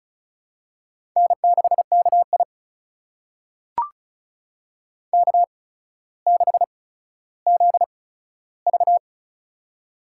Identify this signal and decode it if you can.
Morse code, heard as a keyed tone, about 34 wpm, spaced as for 15 wpm, 700 Hz. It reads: N6KI K6ZV
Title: USA Call Signs - 4 Characters - 8x Character Spacing - 35wpm